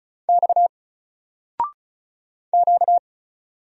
Morse code, heard as a keyed tone, about 36 wpm, 700 Hz. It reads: X Q
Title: Single Letter-Number Alt 35wpm